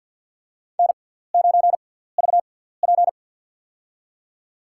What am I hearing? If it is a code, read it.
Morse code, heard as a keyed tone, about 47 wpm, spaced as for 17 wpm, 700 Hz. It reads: N9VP
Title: USA Call Signs - 6x Character Spacing - 50wpm